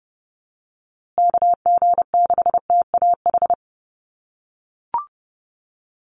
Morse code, heard as a keyed tone, about 30 wpm, 700 Hz. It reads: KG6TAH
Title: USA Call Signs Repeated 3x 30wpm